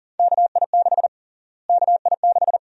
Morse code, heard as a keyed tone, about 40 wpm, 700 Hz. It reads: KI6 KI6
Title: USA Call Sign Prefixes - Repeated 2x - 40wpm